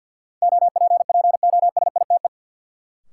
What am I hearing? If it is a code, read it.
Morse code, heard as a keyed tone, about 51 wpm, 700 Hz. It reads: OPPOSITE
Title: Sentences from Top 1300 Words - New Word Drill Alt - 50wpm